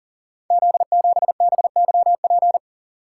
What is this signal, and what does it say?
Morse code, heard as a keyed tone, about 42 wpm, 700 Hz. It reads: Z7BYP
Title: Code Groups - 5 Letter:Numbers - 40wpm